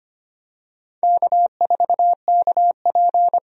Morse code, heard as a keyed tone, about 25 wpm, 700 Hz. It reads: K4KP
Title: USA Call Signs - 4 Characters - 25wpm